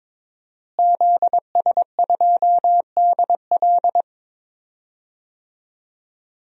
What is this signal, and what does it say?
Morse code, heard as a keyed tone, about 22 wpm, 700 Hz. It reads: ZS2DL
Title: DXCC-100 Call Signs - 5 Characters - Repeated 2x - 22wpm